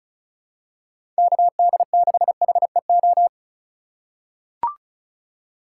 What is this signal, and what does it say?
Morse code, heard as a keyed tone, about 34 wpm, 700 Hz. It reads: KD6HEO
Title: USA Call Signs - 6 Characters - 2x Character Spacing - 35wpm